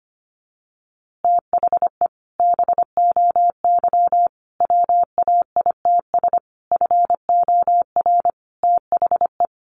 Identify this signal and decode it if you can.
Morse code, heard as a keyed tone, about 25 wpm, 700 Hz. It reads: THE BOY WAITS FOR THE
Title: Easy Phrases 25wpm